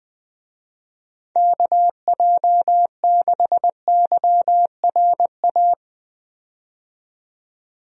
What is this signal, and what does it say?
Morse code, heard as a keyed tone, about 20 wpm, 700 Hz. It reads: KJ6YRA
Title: USA Call Signs - 6 Characters - 20wpm